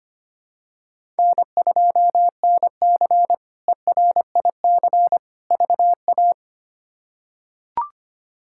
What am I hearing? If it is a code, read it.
Morse code, heard as a keyed tone, about 25 wpm, 700 Hz. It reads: N2NC ERIC VA
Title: K1USN SST QSO Practice - 25wpm